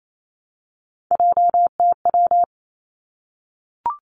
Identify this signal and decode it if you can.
Morse code, heard as a keyed tone, about 28 wpm, 700 Hz. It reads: JTW
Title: Three Challenging Letter-Number Combinations 28wpm